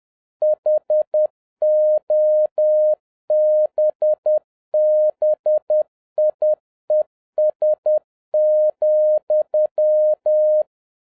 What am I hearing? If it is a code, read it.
Morse code, heard as a keyed tone, about 10 wpm, 610 Hz. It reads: HOBBIES,
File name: 2011-07-mp3 / n06 A LOT HAS CHANGED 10wpm